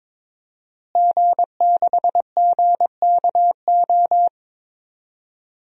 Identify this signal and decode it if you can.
Morse code, heard as a keyed tone, about 22 wpm, 700 Hz. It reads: G6GKO
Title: DXCC-100 Call Signs - 5 Characters - Repeated 3x - 22wpm